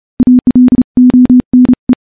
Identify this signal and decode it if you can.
Morse code, heard as a keyed tone, about 31 wpm, 250 Hz. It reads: ALONE
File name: morse / morse-training-30WPM